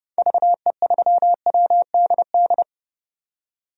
Morse code, heard as a keyed tone, about 29 wpm, 700 Hz. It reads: VE3WDD
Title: DXCC-100 Call Signs - 6 Characters - 30wpm